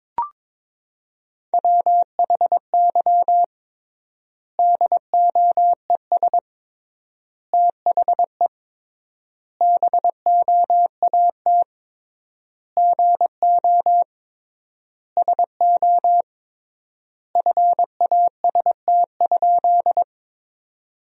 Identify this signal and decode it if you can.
Morse code, heard as a keyed tone, about 22 wpm, 700 Hz. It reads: WHY DOES THE BOAT GO SO FAST?
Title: 3x Word-Spacing - Sentences from Top 500 Words - 22wpm